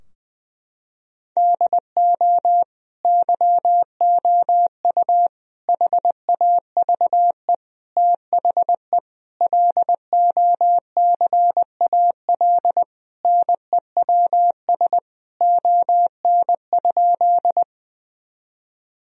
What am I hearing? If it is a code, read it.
Morse code, heard as a keyed tone, about 21 wpm, 700 Hz. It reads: DO YOU HAVE THE LOCAL NEWS ON?
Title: Sentences from Top 1000 Words - Review Part 3 - 20wpm